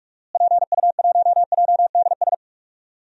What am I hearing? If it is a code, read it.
Morse code, heard as a keyed tone, about 42 wpm, 700 Hz. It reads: PU1JDS
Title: DXCC-100 Call Signs - 6 Characters - Repeated 2x - 45wpm